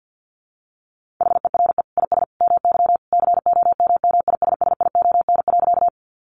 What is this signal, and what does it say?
Morse code, heard as a keyed tone, about 50 wpm, 700 Hz. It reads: HERE IS MY COMMISSION.
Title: Sentences from Top 1400 Words 50wpm